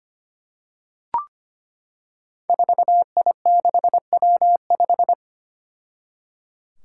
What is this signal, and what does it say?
Morse code, heard as a keyed tone, about 25 wpm, 700 Hz. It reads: 4I6W5
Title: Five Challenging Letter-Number Combinations 25wpm